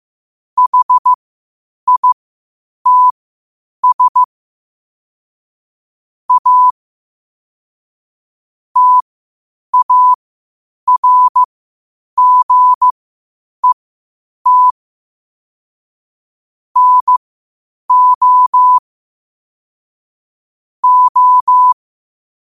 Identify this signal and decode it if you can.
Morse code, heard as a keyed tone, about 15 wpm, spaced as for 8 wpm, 1000 Hz. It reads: HITS A TARGET NO O
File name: qotdfiles / 8wpm